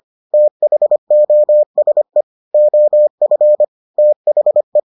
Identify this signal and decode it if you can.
Morse code, heard as a keyed tone, about 25 wpm, 600 Hz. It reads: THOSE OF THE